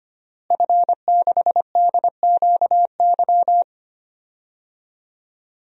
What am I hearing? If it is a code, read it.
Morse code, heard as a keyed tone, about 25 wpm, 700 Hz. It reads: F6DQY
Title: DXCC-100 Call Signs - 5 Characters - Repeated 3x - 25wpm